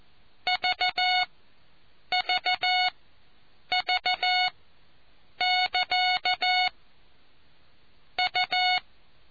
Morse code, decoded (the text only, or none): VVV<KA> U